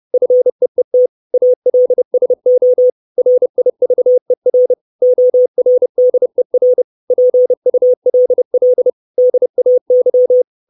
Morse code, heard as a keyed tone, about 30 wpm, 500 Hz. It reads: FEET ALSO RIVER ORDER PULL DAY
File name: 632 Words 30WPM